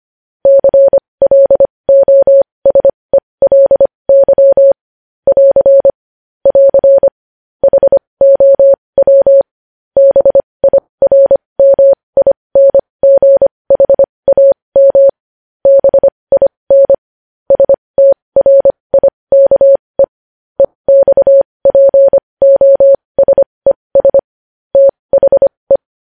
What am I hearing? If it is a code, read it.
Morse code, heard as a keyed tone, about 25 wpm, spaced as for 20 wpm, 550 Hz. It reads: CLOSELY <AR> <AR> HOW BIRMINGHAM BIN STRIKE EXPOSES THE
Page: news-headlines-in-morse-code-at-20-wpm